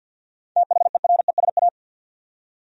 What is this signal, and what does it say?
Morse code, heard as a keyed tone, about 52 wpm, 700 Hz. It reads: THERESA
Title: USA Common Names - 50wpm